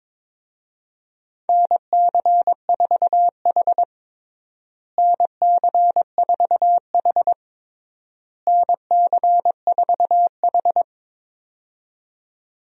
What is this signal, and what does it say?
Morse code, heard as a keyed tone, about 22 wpm, 700 Hz. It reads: NC4H NC4H NC4H
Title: USA Call Signs - 4 Characters - Repeated 3x - 22wpm